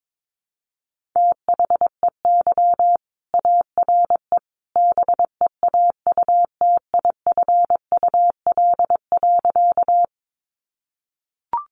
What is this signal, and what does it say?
Morse code, heard as a keyed tone, about 22 wpm, 700 Hz. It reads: THEY ARE BEAUTIFUL.
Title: Sentences from Top 900 Words 22wpm